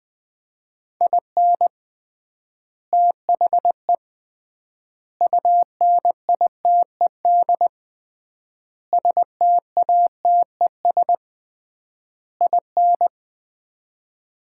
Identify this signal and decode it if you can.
Morse code, heard as a keyed tone, about 20 wpm, 700 Hz. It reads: IN THE UNITED STATES IN
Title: 3x Word-Spacing - Sets of 5 Words - 20wpm